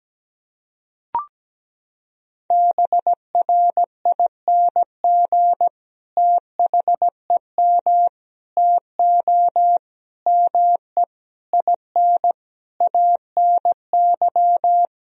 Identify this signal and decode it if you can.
Morse code, heard as a keyed tone, about 17 wpm, 700 Hz. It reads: BRING THEM TO ME IN ANY
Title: Sentences from Top 1000 Words - Review Part 3 - 17wpm